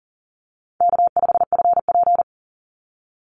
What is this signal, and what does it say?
Morse code, heard as a keyed tone, about 40 wpm, 700 Hz. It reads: K5FP
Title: Easy Call Signs Repeated 3x - 40wpm